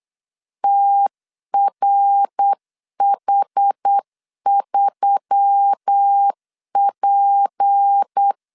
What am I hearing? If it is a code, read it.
Morse code, heard as a keyed tone, about 8 wpm, 800 Hz. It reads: TRH3P
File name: LEZIONI / 7 GF9Z / 7 G 8wpm 3118